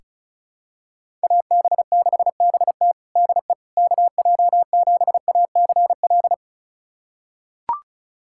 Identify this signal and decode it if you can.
Morse code, heard as a keyed tone, about 36 wpm, 700 Hz. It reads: AB6BT DE KJ7ACL
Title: USA Callsign DE Callsign 35wpm